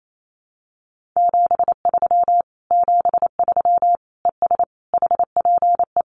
Morse code, heard as a keyed tone, about 28 wpm, 700 Hz. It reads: 73 73 ES HPE
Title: Common QSO Phrases Encore 28wpm